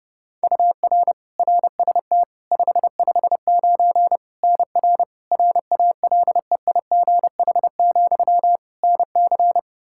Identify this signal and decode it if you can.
Morse code, heard as a keyed tone, about 30 wpm, 700 Hz. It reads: UR RST 559 NR RALEIGH, NC
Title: Practice QSO - Exchange 1 - Beginning - 30wpm